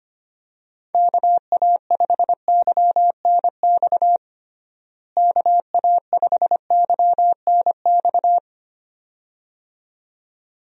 Morse code, heard as a keyed tone, about 25 wpm, 700 Hz. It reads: KA5YNX KA5YNX
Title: USA Call Signs - 6 Characters - Repeated 2x - 25wpm